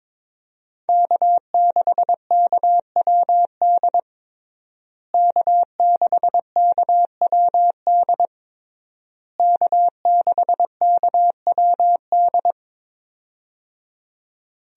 Morse code, heard as a keyed tone, about 22 wpm, 700 Hz. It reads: K6KWD K6KWD K6KWD
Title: USA Call Signs - 5 Characters - Repeated 3x - 22wpm